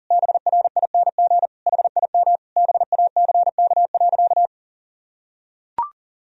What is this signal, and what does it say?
Morse code, heard as a keyed tone, about 40 wpm, 700 Hz. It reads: BRING HIM BACK.
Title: Sentences from Top 1000 Words - Review Part 3 - 40wpm